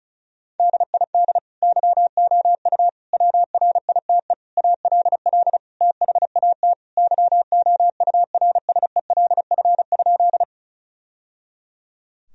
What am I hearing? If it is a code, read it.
Morse code, heard as a keyed tone, about 36 wpm, 700 Hz. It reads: DID YOU WRITE ALL THAT YOURSELF?
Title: Sentences from Top 1000 Words - Review Part 1 - 35wpm